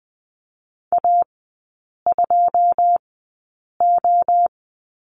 Morse code, heard as a keyed tone, about 20 wpm, spaced as for 8 wpm, 700 Hz. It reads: A2O
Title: 081 Lesson 14e - Number 2 - Three Characters 20wpm